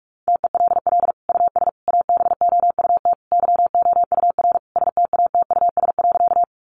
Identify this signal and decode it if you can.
Morse code, heard as a keyed tone, about 44 wpm, 700 Hz. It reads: TELL US ABOUT YOUR STATUS.
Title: Sentences from Top 1400 Words 45wpm